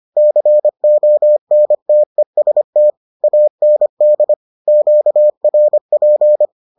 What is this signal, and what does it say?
Morse code, heard as a keyed tone, about 25 wpm, 600 Hz. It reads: CONTEST AND QRP